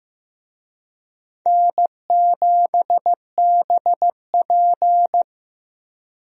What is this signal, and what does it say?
Morse code, heard as a keyed tone, about 15 wpm, 700 Hz. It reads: N7BP
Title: USA Call Signs - 4 Characters - Repeated 2x - 15wpm